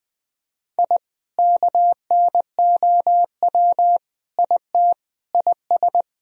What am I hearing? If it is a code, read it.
Morse code, heard as a keyed tone, about 20 wpm, 700 Hz. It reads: I KNOW IT IS
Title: Sentences from Top 1000 Words - Review Part 4 - 20wpm